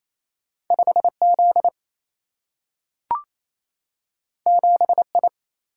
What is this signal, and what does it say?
Morse code, heard as a keyed tone, about 28 wpm, 700 Hz. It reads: 5 Z 7 I
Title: Single-Letter Single-Number 28wpm